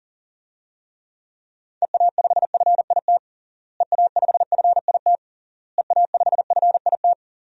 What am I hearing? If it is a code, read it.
Morse code, heard as a keyed tone, about 40 wpm, 700 Hz. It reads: EA5FIT EA5FIT EA5FIT
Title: DXCC-100 Call Signs - 6 Characters - Repeated 3x - 40wpm